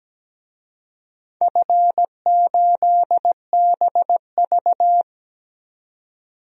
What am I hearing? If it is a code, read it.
Morse code, heard as a keyed tone, about 17 wpm, 700 Hz. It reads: F8BV
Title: DXCC-100 Call Signs - 4 Characters - 17wpm